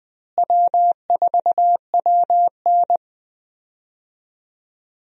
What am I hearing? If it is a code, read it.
Morse code, heard as a keyed tone, about 20 wpm, 700 Hz. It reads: W4WN
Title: USA Call Signs - 4 Characters - Repeated 2x - 20wpm